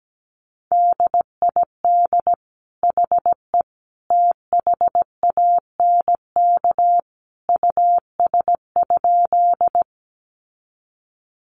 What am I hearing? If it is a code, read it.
Morse code, heard as a keyed tone, about 17 wpm, 700 Hz. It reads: DID HE THANK US?
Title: Sentences from Top 600 Words 17wpm